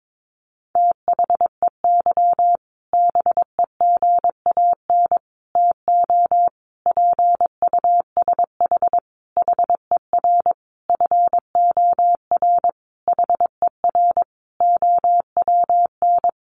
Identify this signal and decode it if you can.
Morse code, heard as a keyed tone, about 22 wpm, 700 Hz. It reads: THEY BEGAN TO PUSH HER FOR HER OWN
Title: Sentences from Top 1300 Words 22wpm